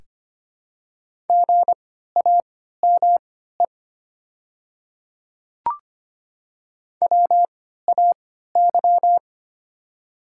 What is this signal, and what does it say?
Morse code, heard as a keyed tone, about 25 wpm, spaced as for 14 wpm, 700 Hz. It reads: GAME WAY
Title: 3x Character Spacing - Top 500 Words - 25wpm